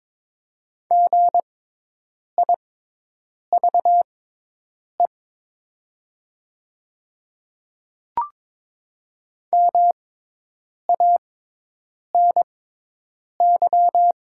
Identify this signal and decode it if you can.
Morse code, heard as a keyed tone, about 22 wpm, spaced as for 8 wpm, 700 Hz. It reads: GIVE MANY
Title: 6x Character Spacing - Top 300 Words - 22wpm